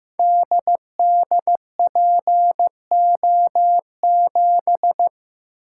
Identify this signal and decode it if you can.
Morse code, heard as a keyed tone, about 15 wpm, 700 Hz. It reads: DDPO7
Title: Code Groups - 5 Letter:Numbers - 15wpm